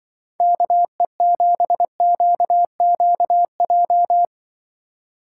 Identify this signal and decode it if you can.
Morse code, heard as a keyed tone, about 24 wpm, 700 Hz. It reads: KE7QQJ
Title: Speed Racing - USA 2x3 Call Signs 20wpm